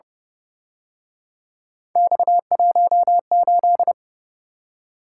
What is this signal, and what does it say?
Morse code, heard as a keyed tone, about 29 wpm, 700 Hz. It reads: X18
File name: Speed-Racing - Three Letter-Number Combinations - 25wpm